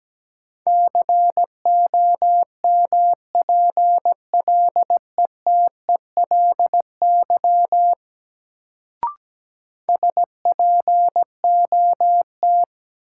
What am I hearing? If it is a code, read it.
Morse code, heard as a keyed tone, about 17 wpm, 700 Hz. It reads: COMPLETELY SPOT
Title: Sentences from Top 1100 Words - New Word Drill Alt - 17wpm